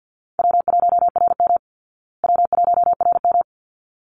Morse code, heard as a keyed tone, about 50 wpm, 700 Hz. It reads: W1RM W1RM
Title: Easy Call Signs Repeated 2x - 50wpm